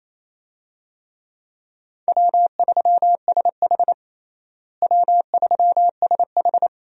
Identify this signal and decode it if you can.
Morse code, heard as a keyed tone, about 27 wpm, 700 Hz. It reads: W3SH W3SH
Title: USA Call Signs - 4 Characters - Repeated 3x - 28wpm